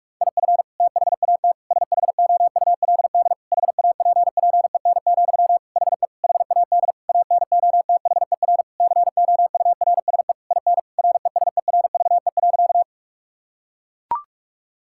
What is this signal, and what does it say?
Morse code, heard as a keyed tone, about 41 wpm, 700 Hz. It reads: IF THAT SHOULD HAPPEN, HE HAD ANOTHER COURSE IN RESERVE.
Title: Sentences from Top 1000 Words - Review Part 6 - 45wpm